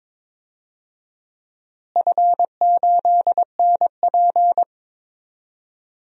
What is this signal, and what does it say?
Morse code, heard as a keyed tone, about 22 wpm, 700 Hz. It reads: F8NP
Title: DXCC-100 Call Signs - 4 Characters - Repeated 3x - 22wpm